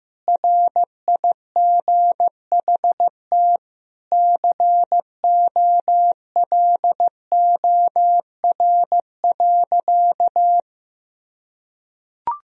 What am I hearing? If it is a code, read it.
Morse code, heard as a keyed tone, about 15 wpm, 700 Hz. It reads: RIGHT COLOR.
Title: Sentences from Top 1000 Words - Review Part 2 - 15wpm